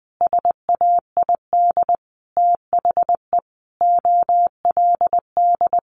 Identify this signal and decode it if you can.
Morse code, heard as a keyed tone, about 20 wpm, 700 Hz. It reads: SAID THE OLD
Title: Sentences from Top 1100 Words 20wpm